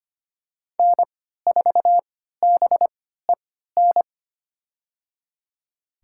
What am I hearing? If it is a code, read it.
Morse code, heard as a keyed tone, about 25 wpm, spaced as for 14 wpm, 700 Hz. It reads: N4BEN
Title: USA Call Signs - 5 Characters - 3x Character Spacing - 25wpm